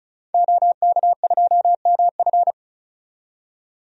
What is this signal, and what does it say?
Morse code, heard as a keyed tone, about 34 wpm, 700 Hz. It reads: OK2MF
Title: DXCC-100 Call Signs - 5 Characters - 35wpm